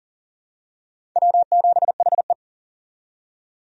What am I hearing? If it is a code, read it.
Morse code, heard as a keyed tone, about 40 wpm, 700 Hz. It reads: W7HE
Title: USA Call Signs - 4 Characters - 40wpm